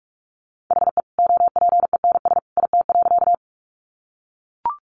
Morse code, heard as a keyed tone, about 44 wpm, 700 Hz. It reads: HE OPENS IT.
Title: Additional Easy Phrases 45wpm